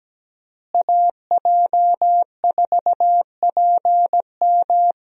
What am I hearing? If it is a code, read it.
Morse code, heard as a keyed tone, about 17 wpm, 700 Hz. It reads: AJ4PM
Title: USA Call Signs - 5 Characters - Repeated 3x - 17wpm